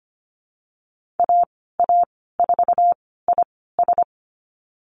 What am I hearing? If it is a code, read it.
Morse code, heard as a keyed tone, about 25 wpm, spaced as for 16 wpm, 700 Hz. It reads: AA4IS
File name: Lesson 15g - Number 5 - Call Signs 25wpm